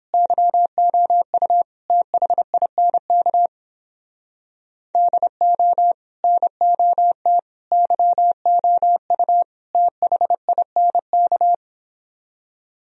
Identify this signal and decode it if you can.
Morse code, heard as a keyed tone, about 29 wpm, 700 Hz. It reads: YOU THINK DO NOT YOU THINK